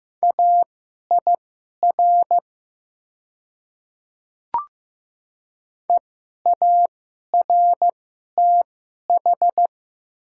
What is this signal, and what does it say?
Morse code, heard as a keyed tone, about 15 wpm, spaced as for 11 wpm, 700 Hz. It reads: AIR EARTH
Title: 2x Character Spacing - Top 300 Words - 15wpm